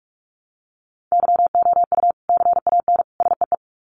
Morse code, heard as a keyed tone, about 44 wpm, 700 Hz. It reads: YOU CAN SEE
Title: Sets of 3 Words 45wpm